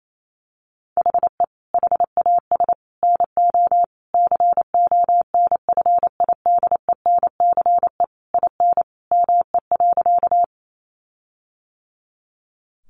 Morse code, heard as a keyed tone, about 28 wpm, 700 Hz. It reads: HE HAS NO CONFIDENCE IN ME.
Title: Sentences from Top 1500 Words 28wpm